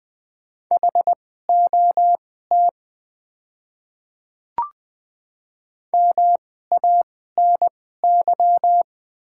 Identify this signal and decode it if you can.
Morse code, heard as a keyed tone, about 20 wpm, spaced as for 14 wpm, 700 Hz. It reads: HOT MANY
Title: 2x Character Spacing - Top 100 Words - 20wpm